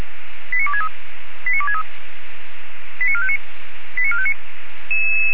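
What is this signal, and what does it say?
Five-tone selective call (ZVEI), heard as quick runs of five tones, tones of about 70 ms. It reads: 89263 89263 89459 89459
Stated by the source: ZVEI five-tone call received on SDR.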